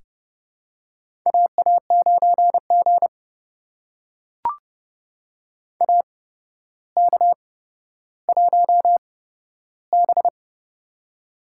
Morse code, heard as a keyed tone, about 30 wpm, spaced as for 13 wpm, 700 Hz. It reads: AA9G AK1B
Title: USA Call Signs - 4 Characters - 8x Character Spacing - 30wpm